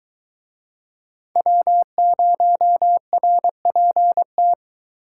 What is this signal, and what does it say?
Morse code, heard as a keyed tone, about 21 wpm, 700 Hz. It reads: W0RPT